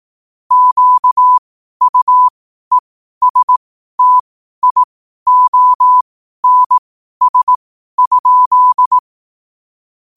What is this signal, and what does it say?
Morse code, heard as a keyed tone, about 18 wpm, spaced as for 12 wpm, 1000 Hz. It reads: QUESTIONS?
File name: qotdfiles / 12wpm